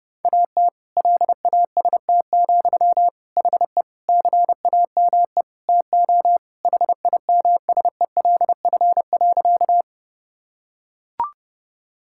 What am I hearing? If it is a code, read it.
Morse code, heard as a keyed tone, about 30 wpm, 700 Hz. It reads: AT LAST, HE CAME TO HIMSELF.
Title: Sentences from Top 1600 Words 30wpm